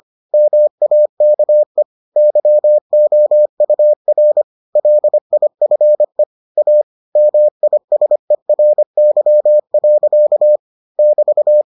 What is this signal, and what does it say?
Morse code, heard as a keyed tone, about 25 wpm, 600 Hz. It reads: MAKE YOUR LIFE A MISERY. <BT>